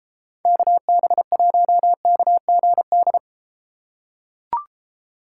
Morse code, heard as a keyed tone, about 32 wpm, 700 Hz. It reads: KB1KGD